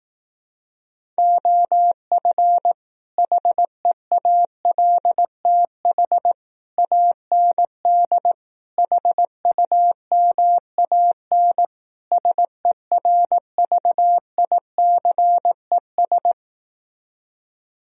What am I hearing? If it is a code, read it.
Morse code, heard as a keyed tone, about 18 wpm, 700 Hz. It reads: OF HEALTH AND HUMAN SERVICES
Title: Speed-Racing - Sets of 5 Words 15wpm